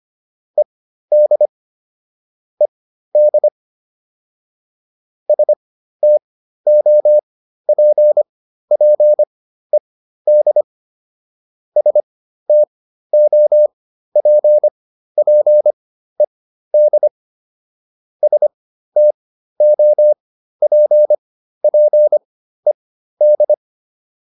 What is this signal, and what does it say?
Morse code, heard as a keyed tone, about 25 wpm, spaced as for 13 wpm, 600 Hz. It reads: ED ED STOPPED STOPPED STOPPED